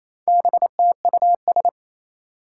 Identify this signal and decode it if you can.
Morse code, heard as a keyed tone, about 27 wpm, 700 Hz. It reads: BTUS
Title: Four Challenging Letter-Number Combinations 28wpm